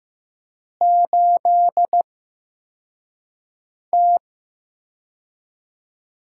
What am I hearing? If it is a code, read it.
Morse code, heard as a keyed tone, about 15 wpm, spaced as for 4 wpm, 700 Hz. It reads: 8T
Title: USA Call Signs - 4 Characters - 8x Character Spacing - 15wpm